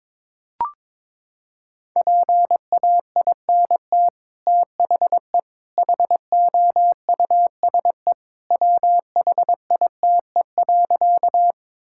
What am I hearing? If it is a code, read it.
Morse code, heard as a keyed tone, about 22 wpm, 700 Hz. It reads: PAINT THE HOUSE WHITE.
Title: Sentences from Top 1000 Words - Review Part 6 - 22wpm